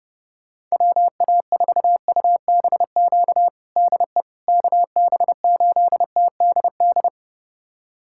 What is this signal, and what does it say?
Morse code, heard as a keyed tone, about 30 wpm, 700 Hz. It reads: WA4UBQ DE KB8TDD
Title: USA Callsign DE Callsign - 6 Characters - 30wpm